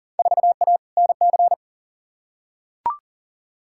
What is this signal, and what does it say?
Morse code, heard as a keyed tone, about 40 wpm, 700 Hz. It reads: 4A NC
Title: Field Day Exchanges Encore - 40wpm